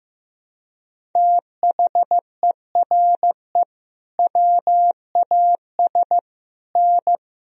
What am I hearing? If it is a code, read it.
Morse code, heard as a keyed tone, about 16 wpm, 700 Hz. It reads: THERE WAS N
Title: Sentences from Top 300 Words and Abbreviations 15wpm